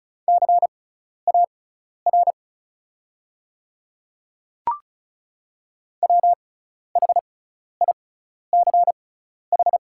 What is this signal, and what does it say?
Morse code, heard as a keyed tone, about 36 wpm, spaced as for 12 wpm, 700 Hz. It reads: CAR WHICH
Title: 6x Character Spacing - Top 300 Words - 35wpm